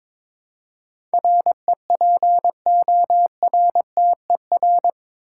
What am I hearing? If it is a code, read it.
Morse code, heard as a keyed tone, about 22 wpm, 700 Hz. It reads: REPORTER
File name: practice / 22wpm / Eight Letter Words - 22wpm